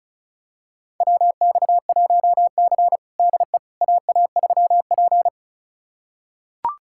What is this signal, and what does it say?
Morse code, heard as a keyed tone, about 34 wpm, 700 Hz. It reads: WX1C DE AA3P
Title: USA Callsign DE Callsign - 4 Characters - 35wpm